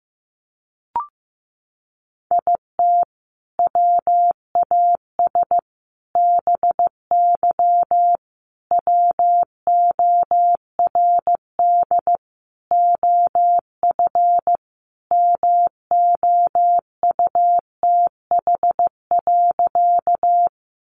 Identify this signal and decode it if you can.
Morse code, heard as a keyed tone, about 14 wpm, 700 Hz. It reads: IT WAS BY WORD OF MOUTH.
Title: Sentences from Top 1500 Words 15wpm